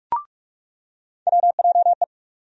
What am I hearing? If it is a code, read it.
Morse code, heard as a keyed tone, about 46 wpm, 700 Hz. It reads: WJE
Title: USA Call Sign Postfixes - 3 Letters 45wpm